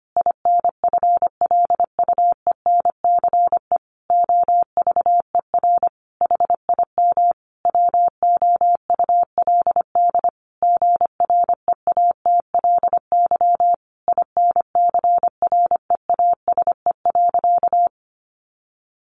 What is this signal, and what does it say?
Morse code, heard as a keyed tone, about 25 wpm, 700 Hz. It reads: INFLUENCE OVER HIM WOULD GREATLY INCREASE.
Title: Sentences from Top 1400 Words 25wpm